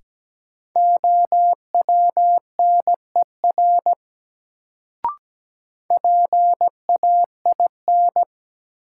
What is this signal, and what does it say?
Morse code, heard as a keyed tone, about 17 wpm, 700 Hz. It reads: OWNER PAIN
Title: Sentences from Top 1600 Words - New Word Drill - 17wpm